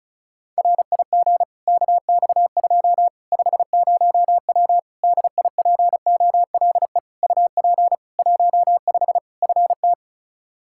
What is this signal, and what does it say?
Morse code, heard as a keyed tone, about 36 wpm, 700 Hz. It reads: RIG KX2 50W DIPOLE UP 15 FT
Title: K1USN SST QSO Practice 35wpm